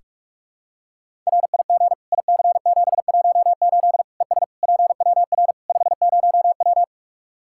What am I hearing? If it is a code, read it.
Morse code, heard as a keyed tone, about 46 wpm, 700 Hz. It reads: RIG IC718 ES PWR 50W
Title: Practice QSO - Exchange 2 - Info - 45wpm